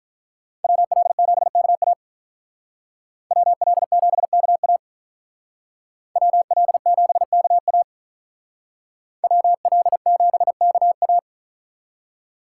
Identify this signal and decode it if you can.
Morse code, heard as a keyed tone, about 44 wpm, 700 Hz. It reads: WL7KA WL7KA WL7KA WL7KA